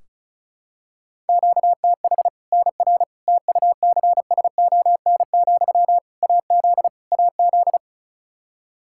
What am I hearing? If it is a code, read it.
Morse code, heard as a keyed tone, about 36 wpm, 700 Hz. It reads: QTH NR TUCSON, AZ AZ
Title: Practice QSO - Exchange 1 - Beginning - 35wpm